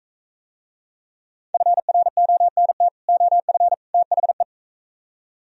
Morse code, heard as a keyed tone, about 43 wpm, 700 Hz. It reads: FRONT OF THE